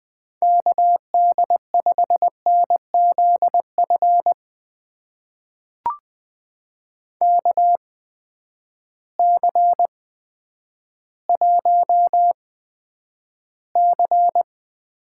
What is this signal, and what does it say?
Morse code, heard as a keyed tone, about 20 wpm, spaced as for 10 wpm, 700 Hz. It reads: KD5NZF KC1C
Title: USA Call Signs - 6 Characters - 8x Character Spacing - 20wpm